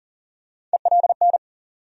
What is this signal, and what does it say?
Morse code, heard as a keyed tone, about 40 wpm, 700 Hz. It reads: ELN
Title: USA Call Sign Postfixes - 3 Letters 40wpm